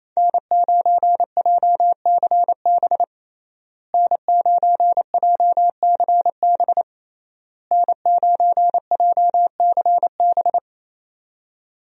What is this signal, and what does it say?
Morse code, heard as a keyed tone, about 28 wpm, 700 Hz. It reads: N9JCB N9JCB N9JCB
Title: USA Call Signs - 5 Characters - Repeated 3x - 28wpm